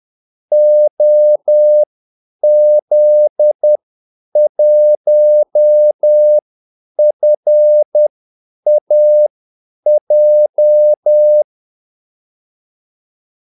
Morse code, heard as a keyed tone, about 10 wpm, spaced as for 8 wpm, 600 Hz. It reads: OZ1FAJ
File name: Callsigns-10-8wpm